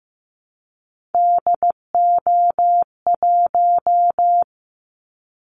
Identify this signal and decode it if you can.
Morse code, heard as a keyed tone, about 15 wpm, 700 Hz. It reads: DO1
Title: Callsign Prefixes for Top 50 DXCC - 15wpm